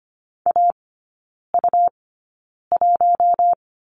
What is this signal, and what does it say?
Morse code, heard as a keyed tone, about 25 wpm, spaced as for 9 wpm, 700 Hz. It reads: AU1
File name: Lesson 17e - Letter u - Three Characters 25wpm